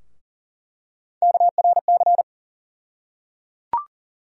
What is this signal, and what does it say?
Morse code, heard as a keyed tone, about 38 wpm, 700 Hz. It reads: KRC
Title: USA Call Sign Postfixes - 3 Letters 40wpm